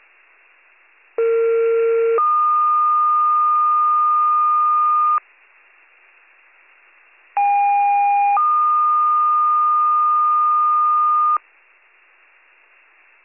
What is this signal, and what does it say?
Two-tone page, heard as a one-second tone, then a three-second tone, about 460 then 1220 Hz, about 810 then 1220 Hz. It is heard from 1.2 to 11.4 s.